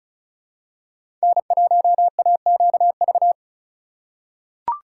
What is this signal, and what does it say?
Morse code, heard as a keyed tone, about 36 wpm, 700 Hz. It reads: N1AQV
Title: USA Call Signs - 4x Character Spacing - 35wpm